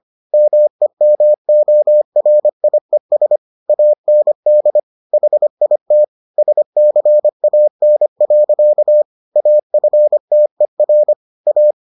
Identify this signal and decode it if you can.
Morse code, heard as a keyed tone, about 25 wpm, 600 Hz. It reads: MEMORIES AND HIT SCAN. AFTER A